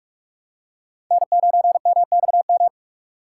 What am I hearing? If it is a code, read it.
Morse code, heard as a keyed tone, about 44 wpm, 700 Hz. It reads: N9MXM